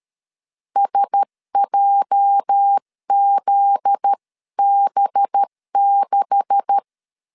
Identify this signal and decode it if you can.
Morse code, heard as a keyed tone, about 13 wpm, 800 Hz. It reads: SJZB6
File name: LEZIONI / 9 286 12wpm 2305